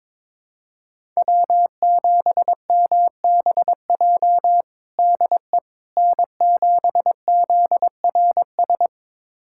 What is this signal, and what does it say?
Morse code, heard as a keyed tone, about 22 wpm, 700 Hz. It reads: W7MBJ DE N7ZRS
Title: USA Callsign DE Callsign - 5 Characters - 22wpm